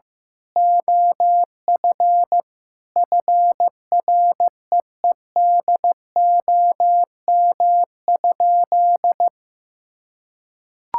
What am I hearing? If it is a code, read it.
Morse code, heard as a keyed tone, about 15 wpm, 700 Hz. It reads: OF FREEDOM?
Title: Sentences from Top 1600 Words 15wpm